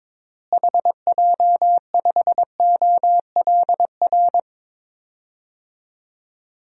Morse code, heard as a keyed tone, about 22 wpm, 700 Hz. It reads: HJ5OLR
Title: DXCC-100 Call Signs - 6 Characters - Repeated 3x - 22wpm